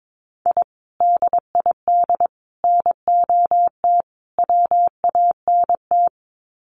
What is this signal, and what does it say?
Morse code, heard as a keyed tone, about 22 wpm, 700 Hz. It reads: I DID NOT WANT
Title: Sets of 4 Words 22wpm